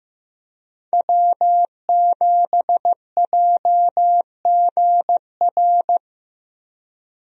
Morse code, heard as a keyed tone, about 15 wpm, 700 Hz. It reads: W7JGR
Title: USA Call Signs - 5 Characters - Repeated 3x - 15wpm